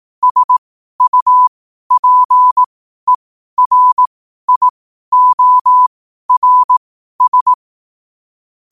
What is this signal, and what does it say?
Morse code, heard as a keyed tone, about 18 wpm, spaced as for 12 wpm, 1000 Hz. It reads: SUPERIORS
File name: qotdfiles / 12wpm